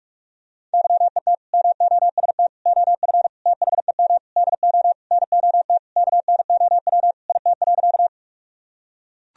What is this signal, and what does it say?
Morse code, heard as a keyed tone, about 41 wpm, 700 Hz. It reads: YET MOST OF THEM DO NOT KNOW IT.
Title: Sentences from Top 1000 Words - Review Part 1 - 45wpm